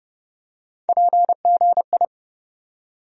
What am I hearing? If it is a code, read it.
Morse code, heard as a keyed tone, about 30 wpm, 700 Hz. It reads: PGI
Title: USA Call Sign Postfixes - 3 Letters 30wpm